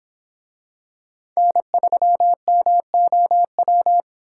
Morse code, heard as a keyed tone, about 25 wpm, 700 Hz. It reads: N3MOW